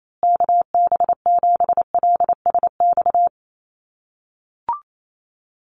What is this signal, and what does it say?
Morse code, heard as a keyed tone, about 28 wpm, 700 Hz. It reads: KB7LSX
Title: USA Call Signs Repeated 2x 28wpm